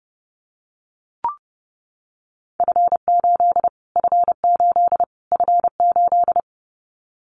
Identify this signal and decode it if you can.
Morse code, heard as a keyed tone, about 30 wpm, 690 Hz. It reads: F8 F8 F8
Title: Callsign Prefixes for Top 25 DXCC - Repeated 3x - 30wpm